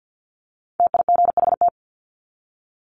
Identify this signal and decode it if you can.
Morse code, heard as a keyed tone, about 50 wpm, 700 Hz. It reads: TIGHT
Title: Top 1751-2000 Words 50wpm